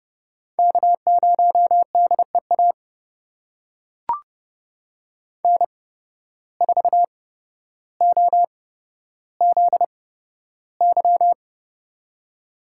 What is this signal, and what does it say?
Morse code, heard as a keyed tone, about 30 wpm, spaced as for 13 wpm, 700 Hz. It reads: K0DEA N4OZY
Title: USA Call Signs - 5 Characters - 8x Character Spacing - 30wpm